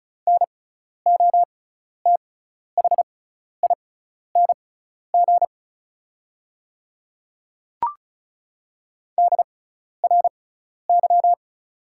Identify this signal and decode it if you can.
Morse code, heard as a keyed tone, about 36 wpm, spaced as for 12 wpm, 690 Hz. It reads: NOTHING DRY
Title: 6x Character Spacing - Top 500 Words - 35wpm